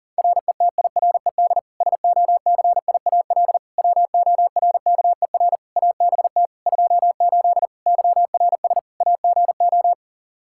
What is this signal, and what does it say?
Morse code, heard as a keyed tone, about 40 wpm, 700 Hz. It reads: RETIRED SOCIAL WORKER ABT 28 YRS AGO
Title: K1USN SST QSO Practice 40wpm